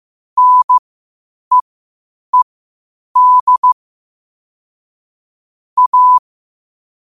Morse code, heard as a keyed tone, about 15 wpm, spaced as for 8 wpm, 1000 Hz. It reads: NEED A
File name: qotdfiles / 8wpm